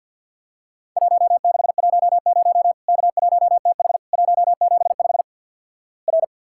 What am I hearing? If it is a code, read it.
Morse code, heard as a keyed tone, about 48 wpm, 700 Hz. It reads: 1610 K1TH 175 R
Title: Worked All Europe DX Contest - QTC Exchange - 50wpm